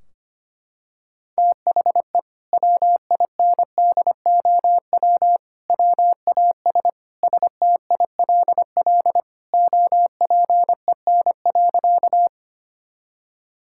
Morse code, heard as a keyed tone, about 25 wpm, 700 Hz. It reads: THE WINDOW WAS STILL OPEN.
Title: Sentences from Top 1600 Words 25wpm